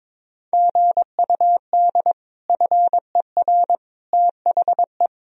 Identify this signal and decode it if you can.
Morse code, heard as a keyed tone, about 22 wpm, 700 Hz. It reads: GUD FER THE
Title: Sentences from Top 300 Words and Abbreviations 22wpm